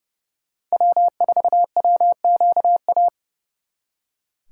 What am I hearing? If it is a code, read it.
Morse code, heard as a keyed tone, about 39 wpm, 700 Hz. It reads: W4WQA